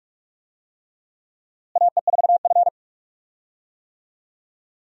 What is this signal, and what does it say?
Morse code, heard as a keyed tone, about 42 wpm, 700 Hz. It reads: AE4F
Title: USA Call Signs - 4 Characters - Repeated 3x - 45wpm